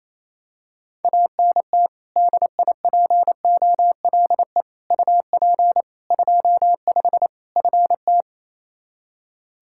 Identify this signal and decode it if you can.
Morse code, heard as a keyed tone, about 28 wpm, 700 Hz. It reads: ANT DIPOLE UP 25 FT
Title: K1USN SST QSO Practice 28wpm